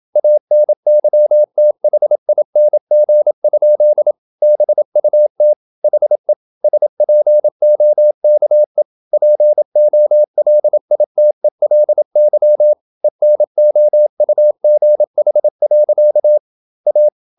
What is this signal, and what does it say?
Morse code, heard as a keyed tone, about 27 wpm, 600 Hz. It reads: ANYTHING? BUT HE SPOKE POLITELY ENOUGH. A
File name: Never 0027wpm